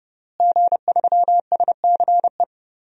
Morse code, heard as a keyed tone, about 30 wpm, 700 Hz. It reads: G3SCE
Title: DXCC-100 Call Signs - 5 Characters - Repeated 2x - 30wpm